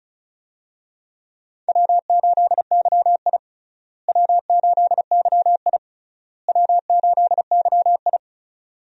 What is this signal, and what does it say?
Morse code, heard as a keyed tone, about 36 wpm, 700 Hz. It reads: W8YI W8YI W8YI
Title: USA Call Signs - 4 Characters - Repeated 3x - 35wpm